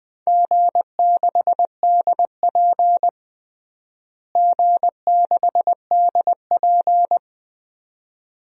Morse code, heard as a keyed tone, about 20 wpm, 700 Hz. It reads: G6DP G6DP
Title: DXCC-100 Call Signs - 4 Characters - Repeated 2x - 20wpm